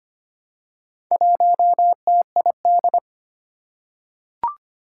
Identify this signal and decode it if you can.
Morse code, heard as a keyed tone, about 25 wpm, 700 Hz. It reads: 1TID
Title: Four Challenging Letter-Number Combinations 25wpm